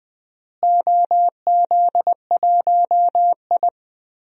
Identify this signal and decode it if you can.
Morse code, heard as a keyed tone, about 20 wpm, 700 Hz. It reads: OZ1I
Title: DXCC-100 Call Signs - 4 Characters - Repeated 2x - 20wpm